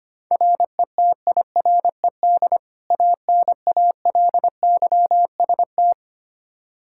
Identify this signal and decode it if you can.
Morse code, heard as a keyed tone, about 25 wpm, 700 Hz. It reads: RETIRED ANALYST
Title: Practice QSO - Exchange 3 - More Info - 25wpm